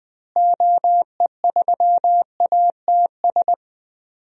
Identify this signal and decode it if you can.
Morse code, heard as a keyed tone, about 20 wpm, 700 Hz. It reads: OE3ATS
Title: DXCC-100 Call Signs - 6 Characters - Repeated 3x - 20wpm